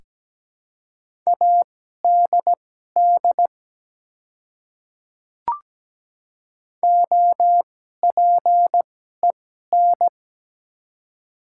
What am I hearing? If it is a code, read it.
Morse code, heard as a keyed tone, about 17 wpm, spaced as for 12 wpm, 700 Hz. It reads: ADD OPEN
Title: 2x Character Spacing - Top 300 Words - 17wpm